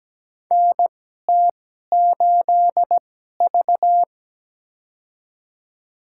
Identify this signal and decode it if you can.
Morse code, heard as a keyed tone, about 17 wpm, spaced as for 12 wpm, 700 Hz. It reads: NT8V
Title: USA Call Signs - 4 Characters - 2x Character Spacing - 17wpm